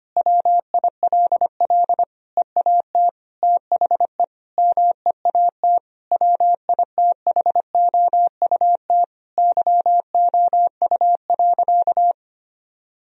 Morse code, heard as a keyed tone, about 25 wpm, 700 Hz. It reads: WILL EAT THE MEAT WITHOUT YOU.
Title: Sentences from Top 1000 Words - Review Part 3 - 25wpm